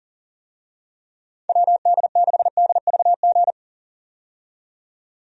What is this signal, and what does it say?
Morse code, heard as a keyed tone, about 40 wpm, 700 Hz. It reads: WD6DVG
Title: USA Call Signs - 6 Characters - 40wpm